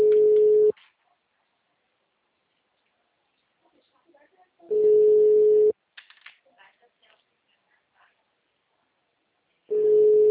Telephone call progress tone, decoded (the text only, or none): ringback tone